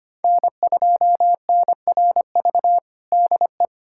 Morse code, heard as a keyed tone, about 25 wpm, 700 Hz. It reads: N2NRV DE
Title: USA Callsign DE Callsign - 5 Characters - 25wpm